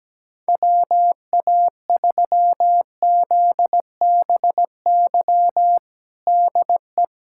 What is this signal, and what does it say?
Morse code, heard as a keyed tone, about 17 wpm, 700 Hz. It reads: WA3ZBY DE
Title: USA Callsign DE Callsign 17wpm